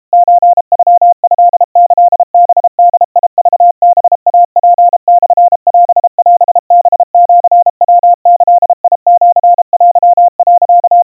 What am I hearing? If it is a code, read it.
Morse code (Japanese Wabun code), heard as a keyed tone, about 32 wpm, 700 Hz. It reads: ソノトキハボクハイツモオオハシヤギシテ、
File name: kawa / Morse sample / kamyu 120 cpm 700 Hz Crispy